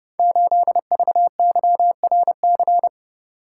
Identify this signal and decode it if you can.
Morse code, heard as a keyed tone, about 28 wpm, 700 Hz. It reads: 8VYRC
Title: Code Groups - 5 Letter:Numbers - 30wpm